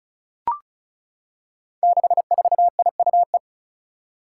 Morse code, heard as a keyed tone, about 36 wpm, 700 Hz. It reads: 64IUE
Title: Five Challenging Letter-Number Combinations 35wpm